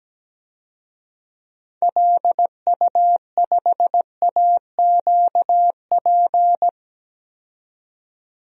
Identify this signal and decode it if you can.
Morse code, heard as a keyed tone, about 17 wpm, 700 Hz. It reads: LU5AQP
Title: DXCC-100 Call Signs - 6 Characters - 17wpm